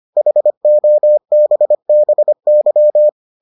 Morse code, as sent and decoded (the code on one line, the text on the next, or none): .... --- -... -... -.--
HOBBY